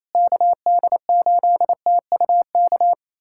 Morse code, heard as a keyed tone, about 28 wpm, 700 Hz. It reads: KD8TUK
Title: USA Callsign DE Callsign 28wpm